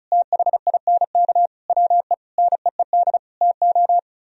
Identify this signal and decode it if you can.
Morse code, heard as a keyed tone, about 43 wpm, 700 Hz. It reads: THINK WE NEED TO
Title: Speed-Racing - Sets of 5 Words 35wpm